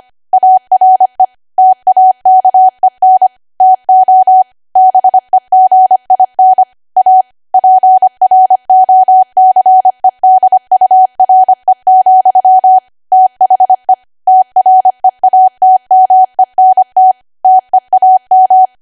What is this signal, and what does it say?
Morse code, heard as a keyed tone, about 25 wpm, 750 Hz. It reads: ARE TAKEN TO BEGIN A PROCEDURE, THE TREATMENT TEAM
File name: Archive / 25 WPM / 25WPM